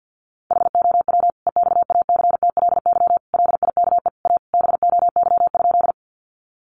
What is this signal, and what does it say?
Morse code, heard as a keyed tone, about 50 wpm, 700 Hz. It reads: HOW EXACTLY LIKE A BOY?
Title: Sentences from Top 900 Words 50wpm